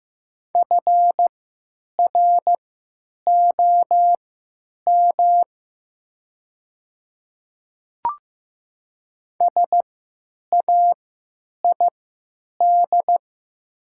Morse code, heard as a keyed tone, about 15 wpm, spaced as for 9 wpm, 700 Hz. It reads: FROM SAID
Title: 3x Character Spacing - Top 300 Words - 15wpm